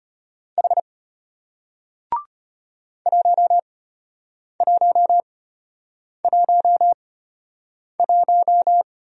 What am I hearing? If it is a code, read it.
Morse code, heard as a keyed tone, about 33 wpm, spaced as for 8 wpm, 700 Hz. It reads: H 1111